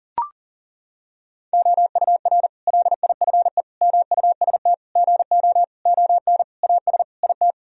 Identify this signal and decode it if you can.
Morse code, heard as a keyed tone, about 40 wpm, 700 Hz. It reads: OUR LIFE MUST GO ON AS IT
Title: Sentences from Top 1000 Words - Review Part 3 - 40wpm